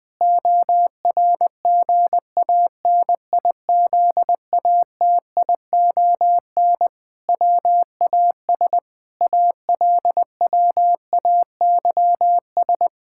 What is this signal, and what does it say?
Morse code, heard as a keyed tone, about 21 wpm, 700 Hz. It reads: ORGANIZATION WAS ALWAYS
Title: Sentences from Top 1000 Words - Review Part 5 - 20wpm